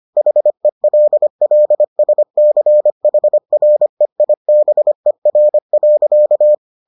Morse code, heard as a keyed tone, about 25 wpm, 600 Hz. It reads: HELLSCHREIBER.